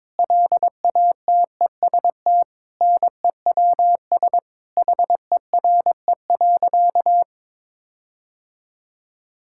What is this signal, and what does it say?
Morse code, heard as a keyed tone, about 22 wpm, 700 Hz. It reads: LATEST NEWS HERE.
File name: Sentences with 9-Word Phrases - 22wpm